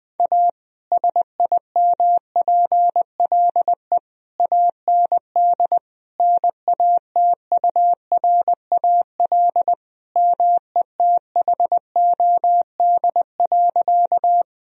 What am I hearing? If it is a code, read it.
Morse code, heard as a keyed tone, about 20 wpm, 700 Hz. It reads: A SIMPLE AND NATURAL METHOD.
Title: Sentences from Top 1000 Words - Review Part 4 - 20wpm